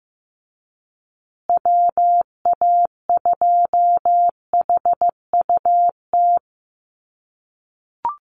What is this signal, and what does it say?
Morse code, heard as a keyed tone, about 15 wpm, 700 Hz. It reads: WA2HUT
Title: USA Call Signs Repeated 2x 15wpm